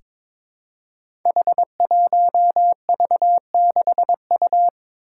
Morse code, heard as a keyed tone, about 22 wpm, 700 Hz. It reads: H1V6U
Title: Five Challenging Letter-Number Combinations 22wpm